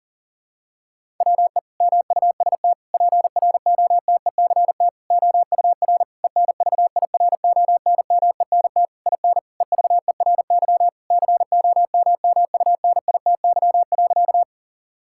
Morse code, heard as a keyed tone, about 40 wpm, 700 Hz. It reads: WE MUST PROTECT OUR ENVIRONMENT IN EVERY COMMUNITY.
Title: Sentences from Top 1000 Words - Review Part 5 - 40wpm